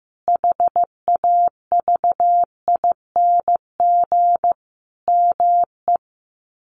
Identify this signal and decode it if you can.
Morse code, heard as a keyed tone, about 14 wpm, 700 Hz. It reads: HAVING ME
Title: Sets of 5 Words Encore 15wpm